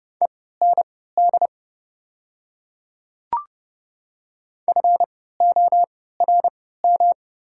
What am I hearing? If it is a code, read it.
Morse code, heard as a keyed tone, about 30 wpm, spaced as for 17 wpm, 690 Hz. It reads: END FORM
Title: 3x Character Spacing - Top 500 Words - 30wpm